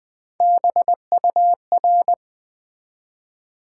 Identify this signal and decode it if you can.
Morse code, heard as a keyed tone, about 20 wpm, 700 Hz. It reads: BUR
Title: Build A Word - 1 Letter At A Time - 4-Letter Words - 20wpm